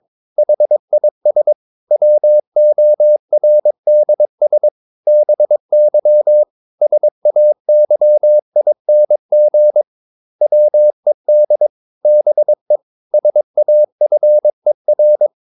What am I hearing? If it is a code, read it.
Morse code, heard as a keyed tone, about 22 wpm, 600 Hz. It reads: HIS WORDS BY SAYING WED BE SAFER
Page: Romanen Never af Ken Follet på engelsk.